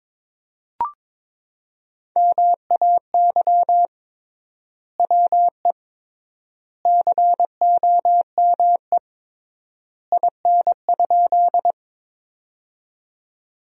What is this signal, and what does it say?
Morse code, heard as a keyed tone, about 22 wpm, 700 Hz. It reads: MAY WE COME IN?
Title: 3x Word-Spacing - Sentences from Top 100 Words - 22wpm